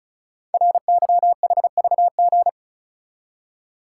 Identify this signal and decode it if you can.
Morse code, heard as a keyed tone, about 36 wpm, 690 Hz. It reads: RYHVG
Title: Code Groups - 5 Letters - 35wpm